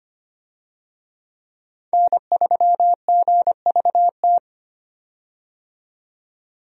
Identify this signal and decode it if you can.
Morse code, heard as a keyed tone, about 25 wpm, 700 Hz. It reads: N3GVT
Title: USA Call Signs - 5 Characters - 25wpm